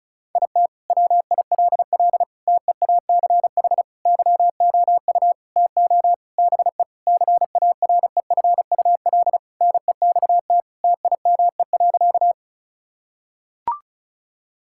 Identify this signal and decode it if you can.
Morse code, heard as a keyed tone, about 34 wpm, 700 Hz. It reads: IT WILL TEACH YOU TO BE CAREFUL NEXT TIME.
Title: Sentences from Top 1600 Words 35wpm